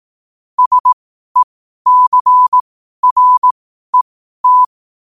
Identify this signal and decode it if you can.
Morse code, heard as a keyed tone, about 18 wpm, spaced as for 12 wpm, 1000 Hz. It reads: SECRET
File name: qotdfiles / 12wpm